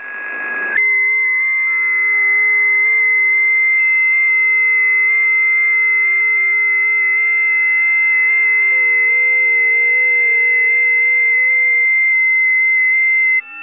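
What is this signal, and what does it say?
FT8.
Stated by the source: FT8 test recording from a busy 20m band.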